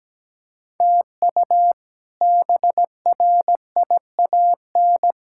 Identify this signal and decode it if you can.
Morse code, heard as a keyed tone, about 17 wpm, 700 Hz. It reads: TU BRIAN
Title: K1USN SST QSO Practice - 17wpm